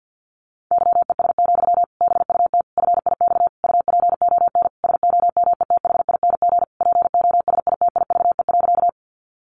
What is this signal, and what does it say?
Morse code, heard as a keyed tone, about 50 wpm, 700 Hz. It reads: YES, BUT FIX UPON SOMETHING POSITIVE.
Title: Sentences from Top 1400 Words 50wpm